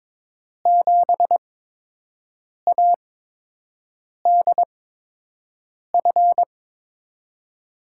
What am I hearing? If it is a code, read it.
Morse code, heard as a keyed tone, about 22 wpm, spaced as for 6 wpm, 700 Hz. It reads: 7ADF
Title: USA Call Signs - 5 Characters - 8x Character Spacing - 22wpm